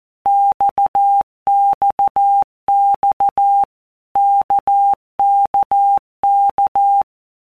Morse code, heard as a keyed tone, about 14 wpm, 800 Hz. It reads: XXX KKK
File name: All Groups 1-minute Blocks / Groups@13 / RKPX